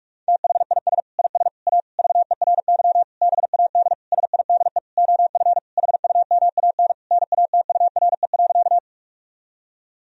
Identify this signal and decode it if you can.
Morse code, heard as a keyed tone, about 46 wpm, 700 Hz. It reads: THIS IS A VERY BAD SIDE OF HUMAN NATURE.
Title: Sentences from Top 1000 Words - Review Part 3 - 45wpm